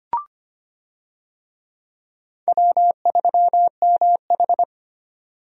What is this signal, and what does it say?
Morse code, heard as a keyed tone, about 25 wpm, 700 Hz. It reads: W3MH
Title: USA Call Signs - 4 Characters - 25wpm